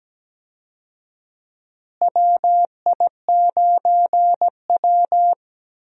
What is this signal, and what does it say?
Morse code, heard as a keyed tone, about 17 wpm, 700 Hz. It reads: WI9W
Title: USA Call Signs - 4 Characters - Repeated 3x - 17wpm